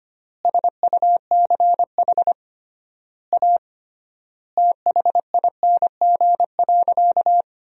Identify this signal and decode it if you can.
Morse code, heard as a keyed tone, about 25 wpm, 700 Hz. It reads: SUCH A THING.
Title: 3x Word-Spacing - Sentences from Top 200 Words - 25wpm